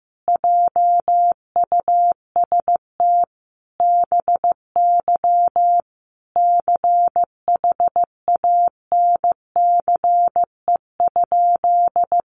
Morse code, heard as a keyed tone, about 15 wpm, 700 Hz. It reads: JUST BY CHANCE?
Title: Sentences from Top 800 Words 15wpm